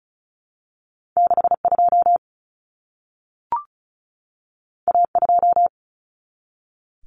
Morse code, heard as a keyed tone, about 36 wpm, 700 Hz. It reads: 62 A2
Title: Common Cut Numbers - 2-Digit 35wpm